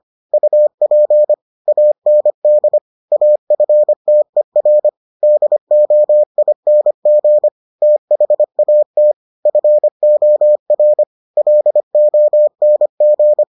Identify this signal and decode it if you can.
Morse code, heard as a keyed tone, about 25 wpm, 600 Hz. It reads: UP AND AFTER DOING THAT FOR LONG